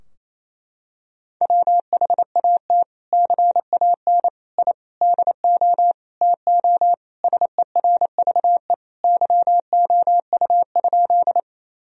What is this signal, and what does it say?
Morse code, heard as a keyed tone, about 28 wpm, 700 Hz. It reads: WHAT CAN I DO TO SERVE YOU?
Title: Sentences from Top 1000 Words - Review Part 2 - 28wpm